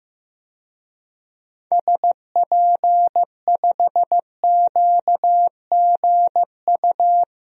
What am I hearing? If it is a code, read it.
Morse code, heard as a keyed tone, about 16 wpm, 700 Hz. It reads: SP5QGU
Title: DXCC-100 Call Signs - 6 Characters - Repeated 3x - 15wpm